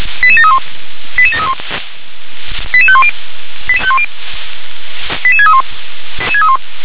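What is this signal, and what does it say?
Five-tone selective call (ZVEI), heard as quick runs of five tones, tones of about 70 ms. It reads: 88731 88731 88510 88631
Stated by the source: Five-tone ZVEI selective call received with an SDR.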